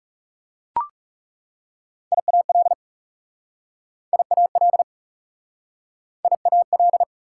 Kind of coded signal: Morse code

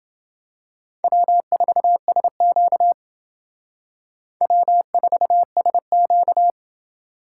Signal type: Morse code

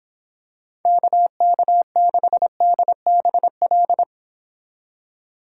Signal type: Morse code

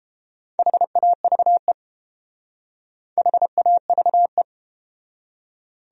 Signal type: Morse code

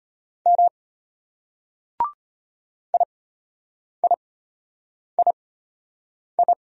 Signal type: Morse code

